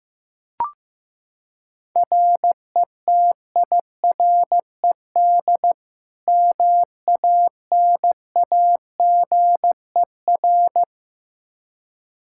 Morse code, heard as a keyed tone, about 15 wpm, 700 Hz. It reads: RETIRED MANAGER
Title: Practice QSO - Exchange 3 - More Info - 15wpm